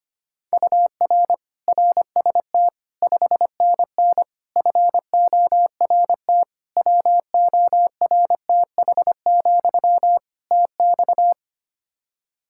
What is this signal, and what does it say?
Morse code, heard as a keyed tone, about 25 wpm, 700 Hz. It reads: UR RST 5NN FORT WORTH, TX
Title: Practice QSO - Exchange 1 - Beginning - 25wpm